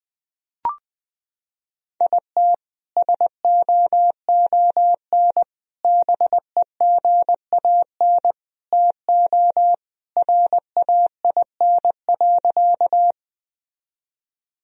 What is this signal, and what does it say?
Morse code, heard as a keyed tone, about 19 wpm, 700 Hz. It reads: IT SOON BEGAN TO RAIN.
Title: Sentences from Top 1000 Words - Review Part 1 - 20wpm